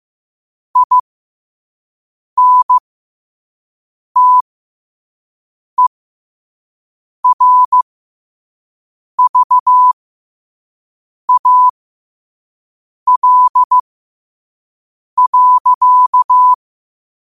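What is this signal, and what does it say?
Morse code, heard as a keyed tone, about 15 wpm, spaced as for 5 wpm, 1000 Hz. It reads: INTERVAL.
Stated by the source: Morse code quote of the day sent at 5 wpm.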